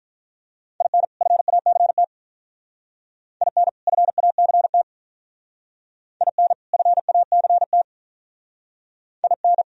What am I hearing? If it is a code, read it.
Morse code, heard as a keyed tone, about 45 wpm, 700 Hz. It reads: IN FACT IN FACT IN FACT IN